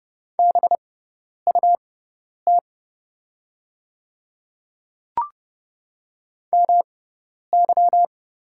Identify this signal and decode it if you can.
Morse code, heard as a keyed tone, about 30 wpm, spaced as for 10 wpm, 700 Hz. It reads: BUT MY
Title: 6x Character Spacing - Top 200 Words - 30wpm